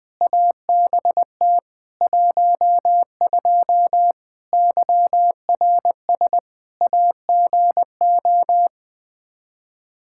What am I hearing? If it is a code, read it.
Morse code, heard as a keyed tone, about 20 wpm, 700 Hz. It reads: ABT 12 YRS AGO
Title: Practice QSO - Exchange 3 - More Info - 20wpm